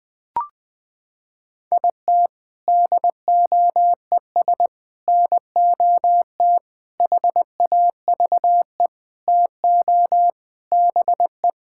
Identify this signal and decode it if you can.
Morse code, heard as a keyed tone, about 20 wpm, 700 Hz. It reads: IT DOES NOT HAVE TO BE
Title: Sentences with 5-Word Phrases - 20wpm